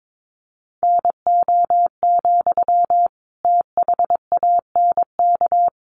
Morse code, heard as a keyed tone, about 22 wpm, 700 Hz. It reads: NO, THANK
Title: Sentences from Top 600 Words 22wpm